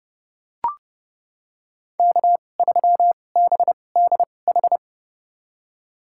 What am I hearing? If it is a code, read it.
Morse code, heard as a keyed tone, about 30 wpm, spaced as for 22 wpm, 700 Hz. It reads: K3BDH
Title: USA Call Signs - 5 Characters - 2x Character Spacing - 30wpm